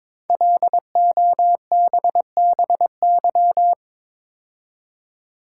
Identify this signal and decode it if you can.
Morse code, heard as a keyed tone, about 22 wpm, 700 Hz. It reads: LOBBY
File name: 5 Letter Words - First 3 Letters Given - 22wpm